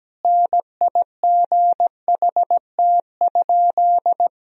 Morse code, heard as a keyed tone, about 17 wpm, 700 Hz. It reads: NIGHT?
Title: Sentences from Top 1000 Words - Review Part 4 - 17wpm